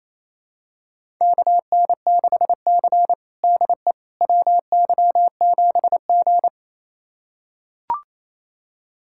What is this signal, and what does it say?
Morse code, heard as a keyed tone, about 28 wpm, 700 Hz. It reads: KN6C DE WY7G
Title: USA Callsign DE Callsign - 4 Characters - 28wpm